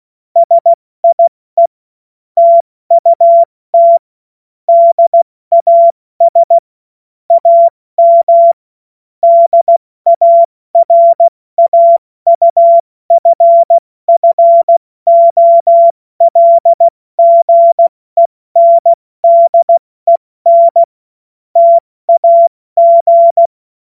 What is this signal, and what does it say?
Morse code, heard as a keyed tone, about 16 wpm, spaced as for 14 wpm, 680 Hz. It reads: SIE TUT DAS AM DARAUFFOLGENDEN TAG
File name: Morsekurs / Witze / Witz 12 März 2024 70 F80